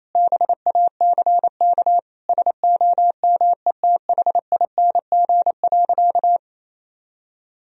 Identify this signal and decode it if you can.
Morse code, heard as a keyed tone, about 28 wpm, 700 Hz. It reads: BACK SOMETHING.
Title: Sentences from Top 1000 Words - Review Part 3 - 28wpm